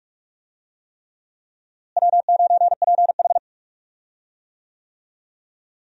Morse code, heard as a keyed tone, about 46 wpm, 700 Hz. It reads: W9PH
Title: USA Call Signs - 4 Characters - Repeated 3x - 45wpm